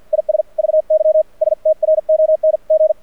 Morse code, heard as a keyed tone, about 48 wpm, spaced as for 42 wpm, 620 Hz. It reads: IS VY STRONG
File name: archived / CRSnet